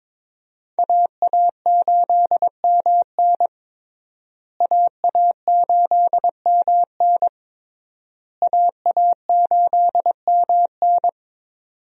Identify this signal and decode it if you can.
Morse code, heard as a keyed tone, about 22 wpm, 700 Hz. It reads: AA8MN AA8MN AA8MN
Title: USA Call Signs - 5 Characters - Repeated 3x - 22wpm